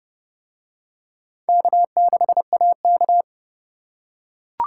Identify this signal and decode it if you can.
Morse code, heard as a keyed tone, about 28 wpm, 700 Hz. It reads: K6AK